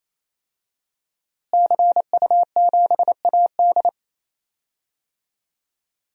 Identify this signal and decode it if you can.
Morse code, heard as a keyed tone, about 28 wpm, 700 Hz. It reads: CU7AD
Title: DXCC-100 Call Signs - 5 Characters - Repeated 3x - 28wpm